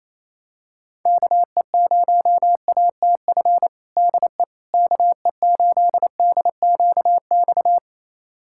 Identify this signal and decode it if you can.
Morse code, heard as a keyed tone, about 28 wpm, 700 Hz. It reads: KE0ATF DE KE8DQX
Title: USA Callsign DE Callsign 28wpm